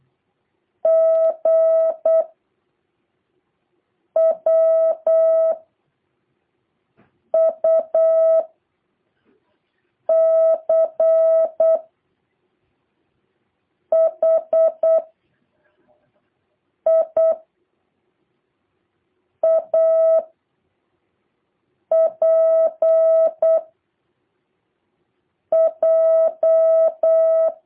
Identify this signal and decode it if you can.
Morse code, heard as a keyed tone, about 8 wpm, 650 Hz. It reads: G W U C H I A P J
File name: jh8cbh / 2023cw